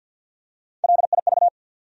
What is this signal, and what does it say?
Morse code, heard as a keyed tone, about 53 wpm, 700 Hz. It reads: LIV
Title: USA Common Names - Max 4 Letters - 50wpm